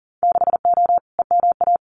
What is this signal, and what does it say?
Morse code, heard as a keyed tone, about 38 wpm, 700 Hz. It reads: 6O EMA
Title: Winter Field Day Exchanges Encore 40wpm